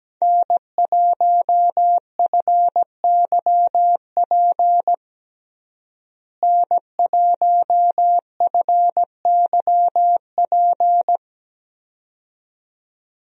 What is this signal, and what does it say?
Morse code, heard as a keyed tone, about 17 wpm, 700 Hz. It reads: N1FYP N1FYP
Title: USA Call Signs - 5 Characters - Repeated 2x - 17wpm